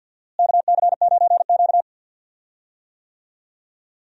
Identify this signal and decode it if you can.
Morse code, heard as a keyed tone, about 48 wpm, 700 Hz. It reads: KC9Q
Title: USA Call Signs - 4 Characters - Repeated 3x - 50wpm